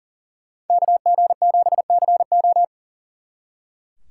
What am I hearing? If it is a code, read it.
Morse code, heard as a keyed tone, about 48 wpm, 700 Hz. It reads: KG7CO
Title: Speed Racing - USA 2x2 Call Signs 40wpm